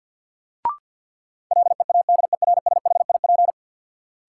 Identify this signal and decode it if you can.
Morse code, heard as a keyed tone, about 53 wpm, 700 Hz. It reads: LEADERSHIP
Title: Sentences from Top 1600 Words - New Word Drill Alt - 50wpm